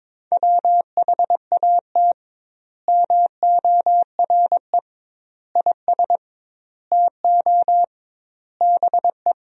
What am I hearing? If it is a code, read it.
Morse code, heard as a keyed tone, about 22 wpm, 700 Hz. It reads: WHAT MORE IS TO BE
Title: 2x Word-Spacing - Sentences from Top 500 Words - 22wpm